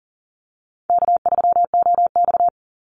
Morse code, heard as a keyed tone, about 40 wpm, 700 Hz. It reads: K3OX
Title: Call Signs Repeated 2x - 40wpm